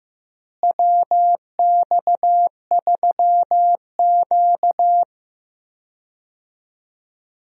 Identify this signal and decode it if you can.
Morse code, heard as a keyed tone, about 15 wpm, 700 Hz. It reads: WX3Q
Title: USA Call Signs - 4 Characters - Repeated 2x - 15wpm